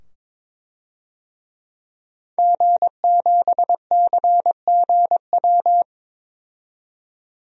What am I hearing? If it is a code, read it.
Morse code, heard as a keyed tone, about 22 wpm, 700 Hz. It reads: G7CGW
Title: DXCC-100 Call Signs - 5 Characters - 22wpm